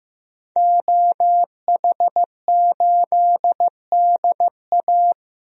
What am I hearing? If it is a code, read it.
Morse code, heard as a keyed tone, about 15 wpm, 700 Hz. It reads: OH8DA
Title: DXCC-100 Call Signs - 5 Characters - Repeated 2x - 15wpm